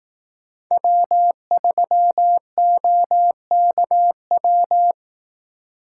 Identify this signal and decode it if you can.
Morse code, heard as a keyed tone, about 18 wpm, 700 Hz. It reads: W3OKW
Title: Speed Racing - USA Call Signs 15wpm